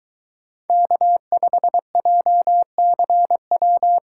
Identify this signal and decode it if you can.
Morse code, heard as a keyed tone, about 23 wpm, 700 Hz. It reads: K5JCW